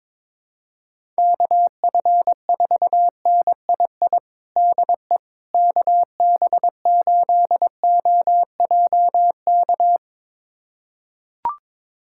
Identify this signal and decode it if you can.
Morse code, heard as a keyed tone, about 22 wpm, 700 Hz. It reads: KF4NII DE KB8OJK
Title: USA Callsign DE Callsign - 6 Characters - 22wpm